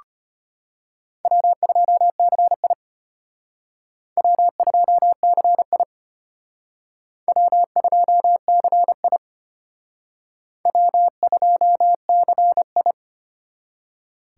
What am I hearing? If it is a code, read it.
Morse code, heard as a keyed tone, about 33 wpm, 700 Hz. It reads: W2CI W2CI W2CI W2CI